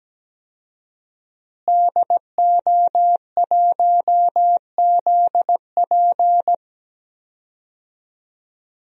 Morse code, heard as a keyed tone, about 17 wpm, 700 Hz. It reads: DO1ZP
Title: DXCC-100 Call Signs - 5 Characters - Repeated 3x - 17wpm